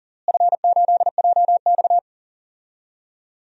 Morse code, heard as a keyed tone, about 40 wpm, 700 Hz. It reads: F8JX
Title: DXCC-100 Call Signs - 4 Characters - Repeated 3x - 40wpm